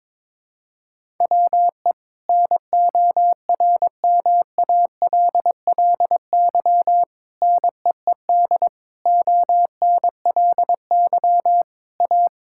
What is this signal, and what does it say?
Morse code, heard as a keyed tone, about 21 wpm, 700 Hz. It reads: WE NORMALLY NEED ONLY A
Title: Sentences from Top 1000 Words - Review Part 6 - 22wpm